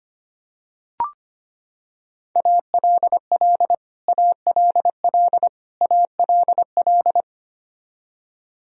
Morse code, heard as a keyed tone, about 25 wpm, 700 Hz. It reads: ALL ALL ALL
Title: Top 100 Words Repeated 25wpm